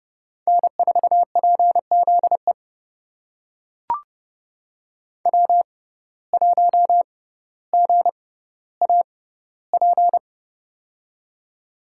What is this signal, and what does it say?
Morse code, heard as a keyed tone, about 31 wpm, spaced as for 15 wpm, 700 Hz. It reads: N4PZE W1GAP
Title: USA Call Signs - 5 Characters - 6x Character Spacing - 30wpm